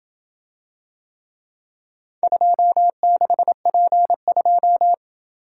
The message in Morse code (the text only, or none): 26P2